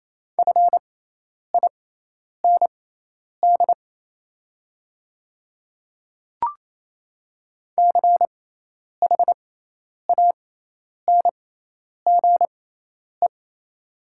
Morse code, heard as a keyed tone, about 28 wpm, spaced as for 10 wpm, 700 Hz. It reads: FIND CHANGE
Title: 6x Character Spacing - Top 200 Words - 28wpm